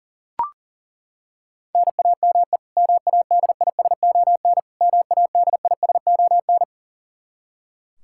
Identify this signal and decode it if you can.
Morse code, heard as a keyed tone, about 40 wpm, 700 Hz. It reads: NAME MADISON MADISON
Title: K1USN SST QSO Practice 40wpm